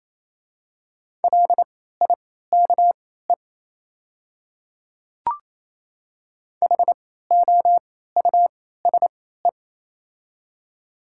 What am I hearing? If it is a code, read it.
Morse code, heard as a keyed tone, about 27 wpm, spaced as for 16 wpm, 700 Hz. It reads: LIKE HOUSE
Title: 3x Character Spacing - Top 200 Words - 28wpm